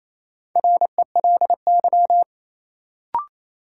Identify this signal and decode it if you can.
Morse code, heard as a keyed tone, about 28 wpm, 700 Hz. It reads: RELY
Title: Sentences from Top 1200 Words - New Word Drill - 28wpm